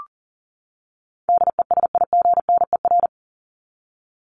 Morse code, heard as a keyed tone, about 40 wpm, 700 Hz. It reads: DESIGNER
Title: Top 1501-1750 Words 40wpm